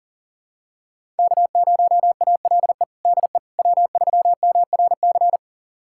Morse code, heard as a keyed tone, about 40 wpm, 700 Hz. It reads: K0ALE DE W3MRC
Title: USA Callsign DE Callsign - 5 Characters - 40wpm